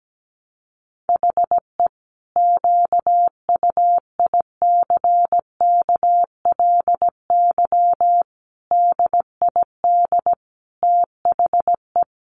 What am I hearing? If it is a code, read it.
Morse code, heard as a keyed tone, about 17 wpm, 700 Hz. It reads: HE QUICKLY DID THE
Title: Sentences from Top 700 Words 17wpm